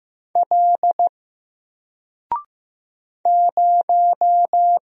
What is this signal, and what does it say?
Morse code, heard as a keyed tone, about 15 wpm, 700 Hz. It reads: L 0
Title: Single Letter-Number Alt 15wpm